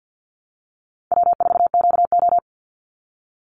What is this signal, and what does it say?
Morse code, heard as a keyed tone, about 48 wpm, 700 Hz. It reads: W4QO
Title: Easy Call Signs Repeated 3x - 50wpm